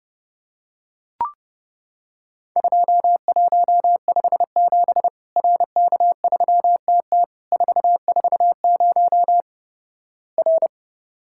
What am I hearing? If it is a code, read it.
Morse code, heard as a keyed tone, about 30 wpm, 700 Hz. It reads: 2157 RK3TT 440 R
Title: Worked All Europe DX Contest - QTC Exchange - 30wpm